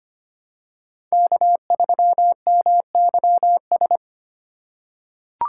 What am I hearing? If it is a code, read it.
Morse code, heard as a keyed tone, about 25 wpm, 700 Hz. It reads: K3MYS
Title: USA Call Signs - 5 Characters - 2x Character Spacing - 25wpm